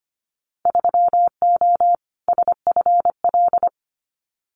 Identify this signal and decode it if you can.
Morse code, heard as a keyed tone, about 25 wpm, 700 Hz. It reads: 3O SFL
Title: Winter Field Day Exchanges 25wpm